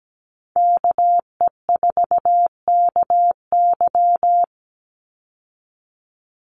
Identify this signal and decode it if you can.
Morse code, heard as a keyed tone, about 17 wpm, 700 Hz. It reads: KE4KY
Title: Call Signs Repeated 2x - 17wpm